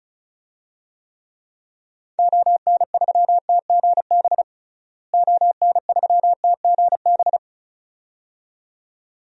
Morse code, heard as a keyed tone, about 36 wpm, 700 Hz. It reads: ON3TGB ON3TGB
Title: DXCC-100 Call Signs - 6 Characters - Repeated 2x - 35wpm